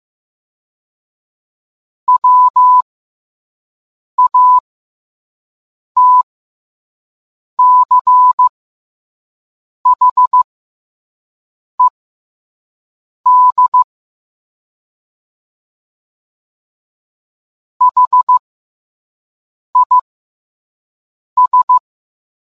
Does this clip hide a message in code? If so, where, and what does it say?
Yes: Morse code from 2.1 to 21.8 s, heard as a keyed tone, about 15 wpm, spaced as for 5 wpm, 1000 Hz. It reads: WATCHED HIS